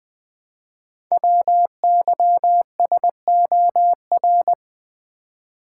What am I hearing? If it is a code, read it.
Morse code, heard as a keyed tone, about 20 wpm, 700 Hz. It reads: WYSOR
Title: Code Groups - 5 Letters - 20wpm